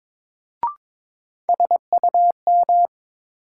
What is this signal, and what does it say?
Morse code, heard as a keyed tone, about 22 wpm, 700 Hz. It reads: SUM
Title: Sentences from Top 1000 Words - Review Part 6 - New Word Drill - 22wpm